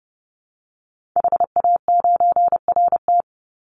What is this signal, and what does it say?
Morse code, heard as a keyed tone, about 30 wpm, 700 Hz. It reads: HA9RT
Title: Call Signs - 30wpm